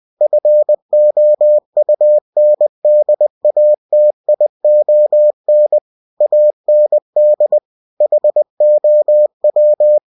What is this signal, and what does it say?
Morse code, heard as a keyed tone, about 20 wpm, 600 Hz. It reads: FOUNDATION AND HOW
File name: cw-books / winnie-the-pooh / winnie-20wpm-page-0120